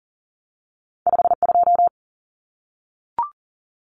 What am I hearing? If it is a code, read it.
Morse code, heard as a keyed tone, about 40 wpm, 700 Hz. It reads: 52
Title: Numbers - 2-Digit 40wpm